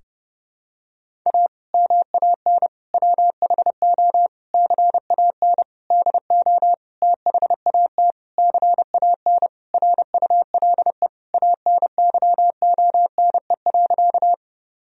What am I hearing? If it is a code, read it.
Morse code, heard as a keyed tone, about 30 wpm, 700 Hz. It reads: A MAN WHO CAN DO THAT CAN RULE ANYONE.
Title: Sentences from Top 1000 Words - Review Part 2 - 30wpm